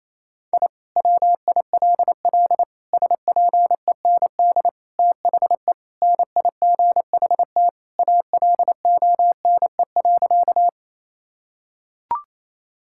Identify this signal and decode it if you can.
Morse code, heard as a keyed tone, about 28 wpm, 700 Hz. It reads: I WILL SPEND THE NIGHT ALONE.
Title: Sentences from Top 1000 Words - Review Part 5 - 28wpm